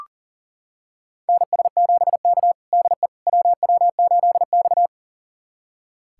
Morse code, heard as a keyed tone, about 40 wpm, 700 Hz. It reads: NS7K DE WW8X
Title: USA Callsign DE Callsign - 4 Characters - 40wpm